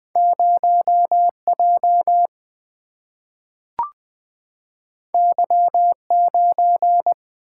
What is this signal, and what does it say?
Morse code, heard as a keyed tone, about 20 wpm, 700 Hz. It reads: 0 J Y 9
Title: Single-Letter Single-Number 20wpm